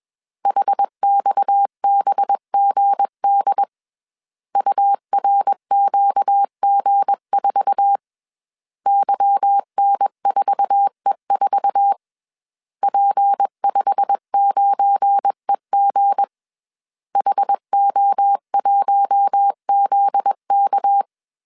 Morse code, heard as a keyed tone, about 21 wpm, 800 Hz. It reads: HXBGD URQG4 YN4E4 P59EG HO1ZK
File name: CW / DET / 1 9 1500C S C 20wpm 2310